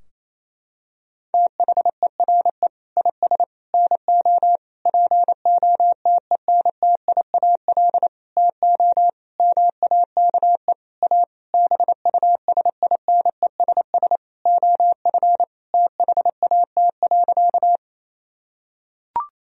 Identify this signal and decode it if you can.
Morse code, heard as a keyed tone, about 28 wpm, 700 Hz. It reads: THERE IS NO POTENTIAL TO MAKE A BUSINESS OF THAT.
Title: Sentences from Top 1000 Words - Review Part 4 - 28wpm